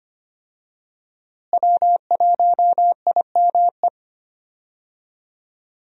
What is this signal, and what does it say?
Morse code, heard as a keyed tone, about 25 wpm, 700 Hz. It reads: W1IME
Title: USA Call Signs - 5 Characters - Repeated 2x - 25wpm